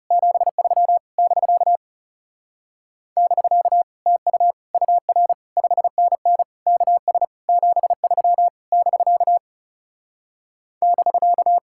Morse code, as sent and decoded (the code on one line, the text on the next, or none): --... ...-- / -...-.- / -...-.- / - ..- / ..- .-. / ..... -. -. / -.- ... / --... ...-- / -...-.- / -...-.-
73 <BK> <BK> TU UR 5NN KS 73 <BK> <BK>